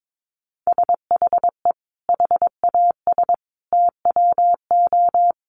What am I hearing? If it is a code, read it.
Morse code, heard as a keyed tone, about 22 wpm, 700 Hz. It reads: SHE HAS TWO
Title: Additional Easy Phrases 22wpm